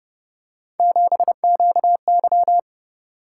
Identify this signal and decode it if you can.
Morse code, heard as a keyed tone, about 31 wpm, 700 Hz. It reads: 7QY
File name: Speed-Racing - Three Letter-Number Combinations - 25wpm